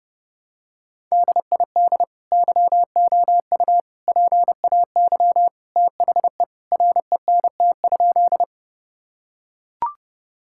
Morse code, heard as a keyed tone, about 30 wpm, 700 Hz. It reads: DID YOU PAY THE RENT?
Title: Sentences from Top 1000 Words - Review Part 5 - 30wpm